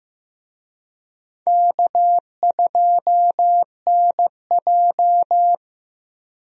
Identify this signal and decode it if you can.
Morse code, heard as a keyed tone, about 15 wpm, 700 Hz. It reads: K2NJ
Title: USA Call Signs - 4 Characters - Repeated 2x - 15wpm